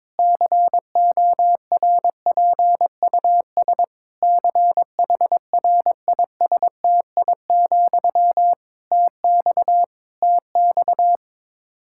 Morse code, heard as a keyed tone, about 22 wpm, 700 Hz. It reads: CORPUS CHRISTI, TX TX
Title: K1USN SST QSO Practice 22wpm